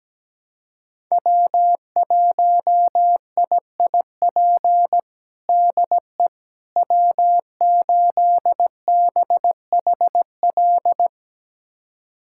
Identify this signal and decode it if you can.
Morse code, heard as a keyed tone, about 17 wpm, 700 Hz. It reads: W1IIP DE W8BHL
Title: USA Callsign DE Callsign - 5 Characters - 17wpm